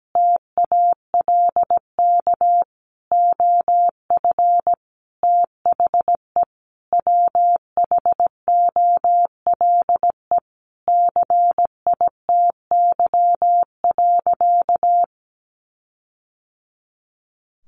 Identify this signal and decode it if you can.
Morse code, heard as a keyed tone, about 17 wpm, 700 Hz. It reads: TALK OF THE WHOLE CITY.
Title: Sentences from Top 1400 Words 17wpm